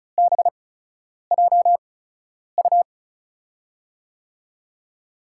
Morse code, heard as a keyed tone, about 36 wpm, spaced as for 10 wpm, 700 Hz. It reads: BJU
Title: USA Call Signs - 8x Character Spacing - 35wpm